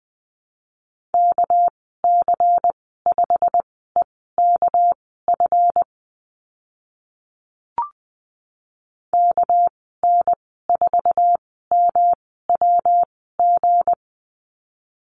Morse code, heard as a keyed tone, about 20 wpm, spaced as for 14 wpm, 700 Hz. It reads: KC5EKF KN4MWG
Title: 219 Lesson 35g - Number 0 - Call Signs 20wpm